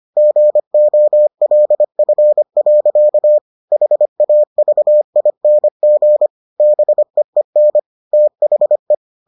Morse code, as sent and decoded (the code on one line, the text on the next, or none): --. --- .-.. ..-. .-.-.- / .... .- ...- .. -. --. / -... . -. / - .... .
GOLF. HAVING BEEN THE